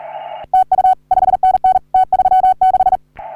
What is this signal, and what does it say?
Morse code, heard as a keyed tone, about 35 wpm, 750 Hz. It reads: TU 5NN T36
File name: CW / HA3OU RHOBCW REC